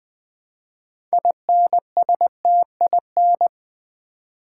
Morse code, heard as a keyed tone, about 20 wpm, 700 Hz. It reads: INSTIN
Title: Build A Word - 2 Letters At A Time - 8-Letter Words - 20wpm